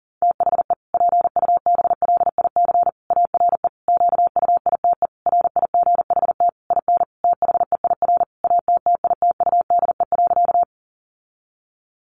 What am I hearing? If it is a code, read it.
Morse code, heard as a keyed tone, about 40 wpm, 700 Hz. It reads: THE PUBLIC ARE QUITE RIGHT IN THEIR ATTITUDE.
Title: Sentences from Top 1300 Words 40wpm